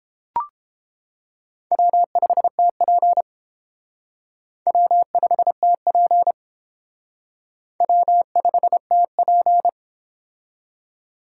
Morse code, heard as a keyed tone, about 29 wpm, 700 Hz. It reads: W5TP W5TP W5TP